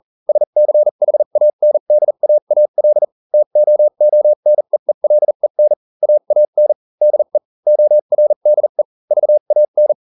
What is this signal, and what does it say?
Morse code, heard as a keyed tone, about 40 wpm, spaced as for 35 wpm, 600 Hz. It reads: SCHANDAAL TOONEELEN AAN DE ORDE VAN